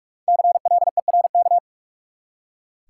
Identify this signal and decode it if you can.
Morse code, heard as a keyed tone, about 46 wpm, 700 Hz. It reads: CLERK
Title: Five Letter Words - 45wpm